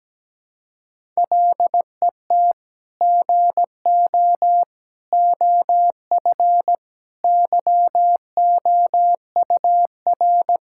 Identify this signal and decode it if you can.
Morse code, heard as a keyed tone, about 17 wpm, 700 Hz. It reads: LET GO OF YOUR
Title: Sentences from Top 1600 Words 17wpm